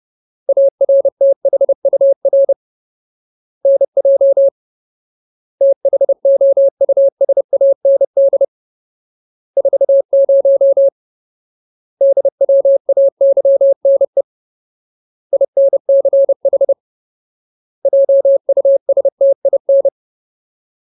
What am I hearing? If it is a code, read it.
Morse code, heard as a keyed tone, about 30 wpm, 550 Hz. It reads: ARTHUR NJ THOUSAND 40 DWAYNE INCH JUSTIN
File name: Ultimate Word List / 30WPM Ultimate